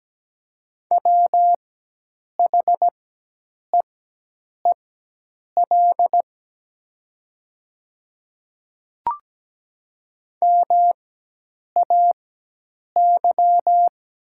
Morse code, heard as a keyed tone, about 17 wpm, spaced as for 8 wpm, 700 Hz. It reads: WHEEL MAY
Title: 4x Character Spacing - Top 500 Words - 17wpm